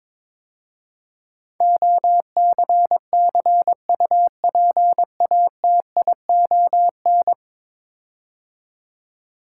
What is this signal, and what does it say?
Morse code, heard as a keyed tone, about 22 wpm, 700 Hz. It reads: OCCUPATION
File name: practice / 22wpm / 10 Letter Words - First 3 Letters Given - 22wpm